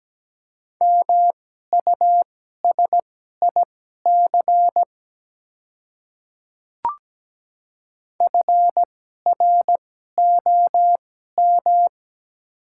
Morse code, heard as a keyed tone, about 17 wpm, spaced as for 12 wpm, 700 Hz. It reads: MUSIC FROM
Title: 2x Character Spacing - Top 300 Words - 17wpm